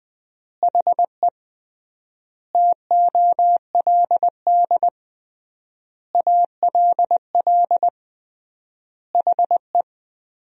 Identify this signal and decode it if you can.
Morse code, heard as a keyed tone, about 20 wpm, 700 Hz. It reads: HE TOLD ALL HE
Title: 3x Word-Spacing - Sentences from Top 400 Words - 20wpm